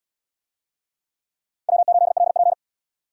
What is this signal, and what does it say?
Morse code, heard as a keyed tone, about 77 wpm, 700 Hz. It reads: W8LP